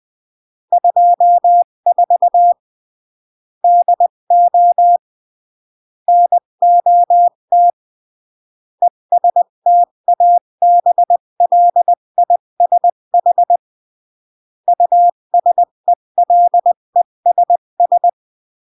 Morse code, heard as a keyed tone, about 20 wpm, 700 Hz. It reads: 24 DO NOT ESTABLISH USELESS
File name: CW / jodorowsky / 82 commandments 20-15